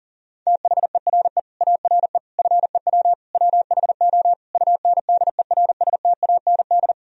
Morse code, heard as a keyed tone, about 40 wpm, 700 Hz. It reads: THERE ARE FEW WHO UNDERSTAND
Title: Sentences from Top 1000 Words - Review Part 1 - 40wpm